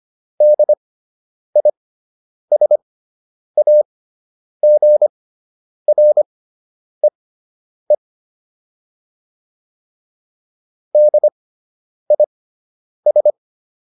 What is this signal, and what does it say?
Morse code, heard as a keyed tone, about 25 wpm, spaced as for 9 wpm, 600 Hz. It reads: DISAGREE DIS